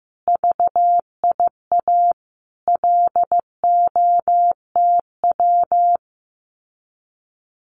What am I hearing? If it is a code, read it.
Morse code, heard as a keyed tone, about 15 wpm, 700 Hz. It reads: VIA LOTW
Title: Common QSO Phrases Encore 15wpm